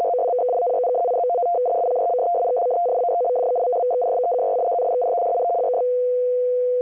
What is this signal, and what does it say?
Two-tone teleprinter signal, about 200 Hz shift.
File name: F03a- -1800-1805utc-7840khz